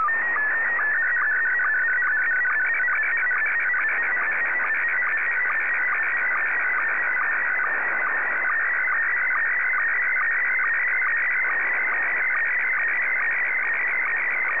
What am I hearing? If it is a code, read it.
SSTV (slow-scan television), Scottie 1.